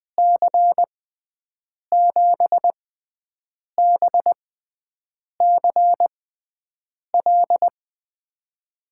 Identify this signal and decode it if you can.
Morse code, heard as a keyed tone, about 20 wpm, spaced as for 7 wpm, 700 Hz. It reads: C7BCL
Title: USA Call Signs - 6 Characters - 6x Character Spacing - 20wpm